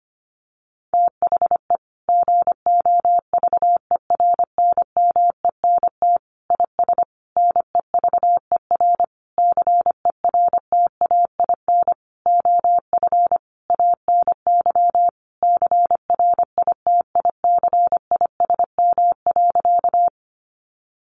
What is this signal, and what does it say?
Morse code, heard as a keyed tone, about 25 wpm, 700 Hz. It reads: THE GOVERNMENT IS NEVER CERTAIN OF ANY CRITICISM.
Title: Sentences from Top 1300 Words 25wpm